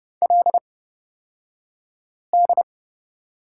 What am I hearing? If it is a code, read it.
Morse code, heard as a keyed tone, about 29 wpm, spaced as for 5 wpm, 700 Hz. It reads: LD
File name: Lesson 38d - Letter q - Two Characters 30wpm